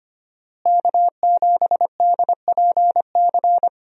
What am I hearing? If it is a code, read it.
Morse code, heard as a keyed tone, about 25 wpm, 700 Hz. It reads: K7DPC
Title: USA Call Signs - 5 Characters - Repeated 2x - 25wpm